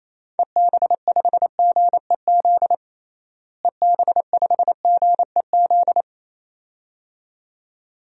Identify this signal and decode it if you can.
Morse code, heard as a keyed tone, about 28 wpm, 700 Hz. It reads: EB5GEZ EB5GEZ
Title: DXCC-100 Call Signs - 6 Characters - Repeated 2x - 28wpm